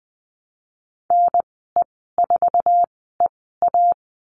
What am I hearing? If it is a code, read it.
Morse code, heard as a keyed tone, about 19 wpm, spaced as for 14 wpm, 700 Hz. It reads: NE4EA
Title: 062 Lesson 11g - Letter h - Call Signs 20wpm